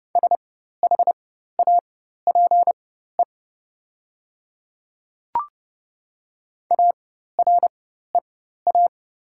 Morse code, heard as a keyed tone, about 30 wpm, spaced as for 14 wpm, 700 Hz. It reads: SHAPE AREA
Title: 4x Character Spacing - Top 500 Words - 30wpm